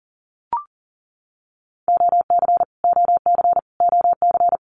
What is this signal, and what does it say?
Morse code, heard as a keyed tone, about 40 wpm, 700 Hz. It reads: OC OC OC
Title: Abbreviations Repeated 3x - 40wpm